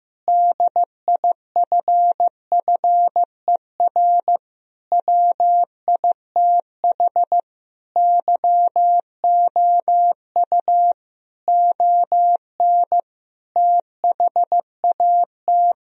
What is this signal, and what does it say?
Morse code, heard as a keyed tone, about 15 wpm, 700 Hz. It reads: DIFFER WITH YOU ON THAT
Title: Sentences from Top 1000 Words - Review Part 1 - 15wpm